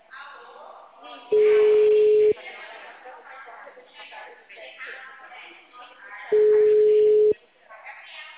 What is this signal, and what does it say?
Ringback tone (Europe, 425 Hz), with speech.